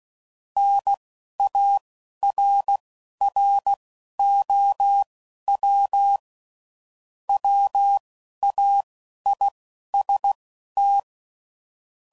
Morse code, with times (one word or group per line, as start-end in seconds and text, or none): 0.6-6.2 NARROW
7.3-11.0 WAIST